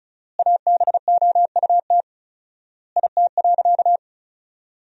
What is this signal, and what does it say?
Morse code, heard as a keyed tone, about 34 wpm, 700 Hz. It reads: ABOUT IT.
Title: 4x Word-Spacing - Sentences from Top 500 Words - 35wpm